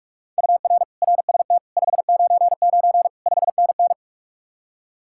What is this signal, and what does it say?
Morse code, heard as a keyed tone, about 49 wpm, 700 Hz. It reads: UR RST 599 5NN